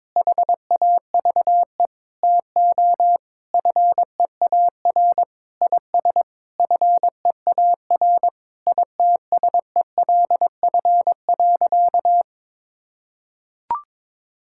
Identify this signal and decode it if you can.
Morse code, heard as a keyed tone, about 22 wpm, 700 Hz. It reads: HAVE TO FEAR IS FEAR ITSELF.
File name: practice / 22wpm / Sentences with 8-Word Phrases - 22wpm